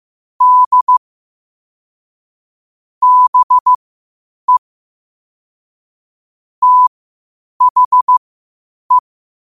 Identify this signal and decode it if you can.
Morse code, heard as a keyed tone, about 15 wpm, spaced as for 8 wpm, 1000 Hz. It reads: D BE THE